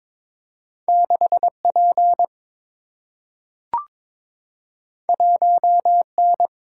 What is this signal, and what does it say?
Morse code, heard as a keyed tone, about 22 wpm, 700 Hz. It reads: 6 P 1 N
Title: Single-Letter Single-Number 22wpm